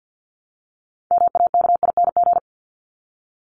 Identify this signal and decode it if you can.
Morse code, heard as a keyed tone, about 50 wpm, 700 Hz. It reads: MAKING
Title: ING Suffix - Top 100 Words with Base Words 50wpm